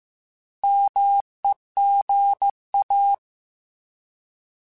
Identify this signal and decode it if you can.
Morse code, heard as a keyed tone, about 15 wpm, 800 Hz. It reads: MEGA
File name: Alphabetical / Alphabetical mp3s / Ms @13